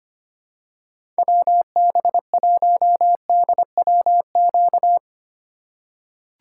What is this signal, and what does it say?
Morse code, heard as a keyed tone, about 25 wpm, 700 Hz. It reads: WB1DWQ
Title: USA Call Signs - 6 Characters - 3x Character Spacing - 25wpm